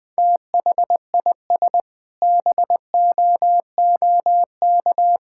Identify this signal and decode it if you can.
Morse code, heard as a keyed tone, about 21 wpm, 700 Hz. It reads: THIS BOOK
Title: Sentences with 6-Word Phrases - Spoken Beginning - 20wpm